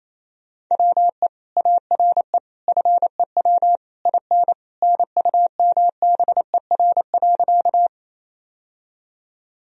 Morse code, heard as a keyed tone, about 27 wpm, 700 Hz. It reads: WE ARE FEW IN NUMBER.
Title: Sentences from Top 1000 Words - Review Part 1 - 28wpm